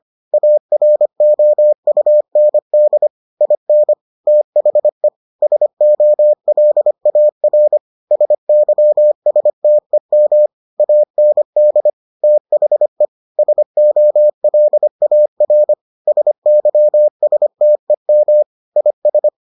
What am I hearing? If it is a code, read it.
Morse code, heard as a keyed tone, about 25 wpm, 600 Hz. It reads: AROUND IN THE SOLAR SYSTEM AND THE SOLAR SYSTEM IS